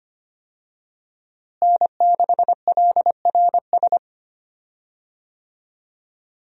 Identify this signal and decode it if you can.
Morse code, heard as a keyed tone, about 25 wpm, 700 Hz. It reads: N6LRS
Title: USA Call Signs - 5 Characters - Repeated 3x - 25wpm